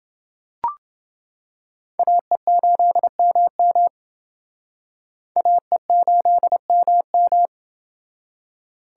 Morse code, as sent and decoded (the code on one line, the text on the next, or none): .- . ---.. -- -- / .- . ---.. -- --
AE8MM AE8MM